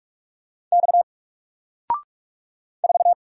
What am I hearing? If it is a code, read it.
Morse code, heard as a keyed tone, about 44 wpm, 700 Hz. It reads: X 4
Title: Single Letter-Number Alt 45wpm